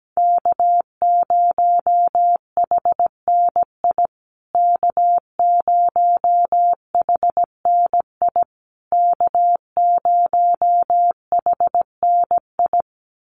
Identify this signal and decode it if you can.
Morse code, heard as a keyed tone, about 17 wpm, 700 Hz. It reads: K0HNI K0HNI K0HNI
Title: USA Call Signs Repeated 3x 17wpm